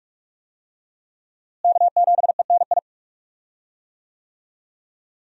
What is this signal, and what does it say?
Morse code, heard as a keyed tone, about 46 wpm, 700 Hz. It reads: K7ENI
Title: USA Call Signs - 5 Characters - Repeated 2x - 45wpm